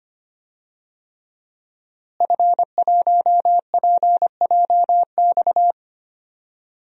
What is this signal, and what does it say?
Morse code, heard as a keyed tone, about 25 wpm, 700 Hz. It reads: F1PJX
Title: DXCC-100 Call Signs - 5 Characters - Repeated 3x - 25wpm